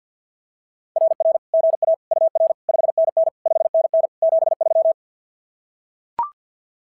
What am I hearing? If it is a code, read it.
Morse code, heard as a keyed tone, about 46 wpm, 660 Hz. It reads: RR GA UR 5NN 5NN 73
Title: SOTA QSO Practice 50wpm